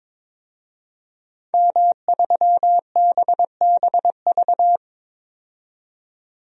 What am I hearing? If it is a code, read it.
Morse code, heard as a keyed tone, about 22 wpm, 700 Hz. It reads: M3BBV
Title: DXCC-100 Call Signs - 5 Characters - 22wpm